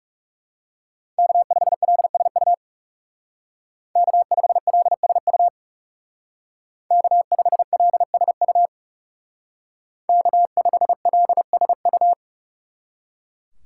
Morse code, heard as a keyed tone, about 39 wpm, 700 Hz. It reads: K5LSU K5LSU K5LSU K5LSU